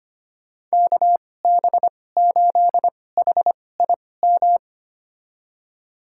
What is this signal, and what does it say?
Morse code, heard as a keyed tone, about 25 wpm, spaced as for 18 wpm, 700 Hz. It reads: KB8HIM
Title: USA Call Signs - 6 Characters - 2x Character Spacing - 25wpm